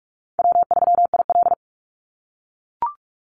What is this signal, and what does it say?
Morse code, heard as a keyed tone, about 44 wpm, 700 Hz. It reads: W3IL
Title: Easy Call Signs - 45wpm